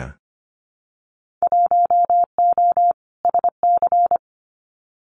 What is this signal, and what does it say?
Morse code, heard as a keyed tone, about 25 wpm, 700 Hz. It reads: 1O SC
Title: Winter Field Day Exchanges 25wpm